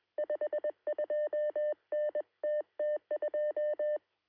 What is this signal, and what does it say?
Morse code, heard as a keyed tone, about 21 wpm, 600 Hz. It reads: 52NTT2
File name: sm7lcb / 1032z sm3bei 23cm